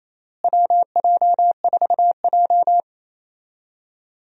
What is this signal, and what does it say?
Morse code, heard as a keyed tone, about 27 wpm, 700 Hz. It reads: WJ4J
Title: USA Call Signs - 4 Characters - Repeated 3x - 28wpm